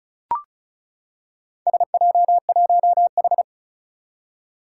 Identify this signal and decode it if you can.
Morse code, heard as a keyed tone, about 36 wpm, 700 Hz. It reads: SJ1H
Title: Four Challenging Letter-Number Combinations 35wpm